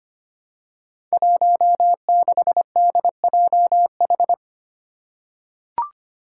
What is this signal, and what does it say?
Morse code, heard as a keyed tone, about 25 wpm, 700 Hz. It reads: 16DJH
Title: Five Challenging Letter-Number Combinations 25wpm